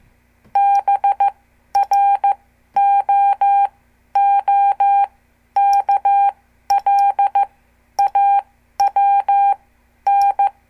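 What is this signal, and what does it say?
Morse code, heard as a keyed tone, about 15 wpm, spaced as for 11 wpm, 800 Hz. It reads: BROOKLAWN
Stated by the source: Example CW QSO between N2APB and N2CX from a CWTD Morse lesson.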